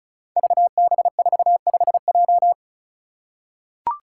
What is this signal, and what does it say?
Morse code, heard as a keyed tone, about 36 wpm, 700 Hz. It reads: VB45J
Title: Code Groups - 5 Letter:Numbers - 35wpm